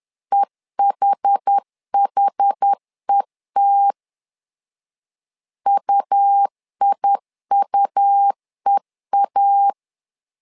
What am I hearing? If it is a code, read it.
Morse code, heard as a keyed tone, about 10 wpm, 800 Hz. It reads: EHHET UIUEA
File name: LEZIONI / 2 UHDN / 2 UH 10wpm 1714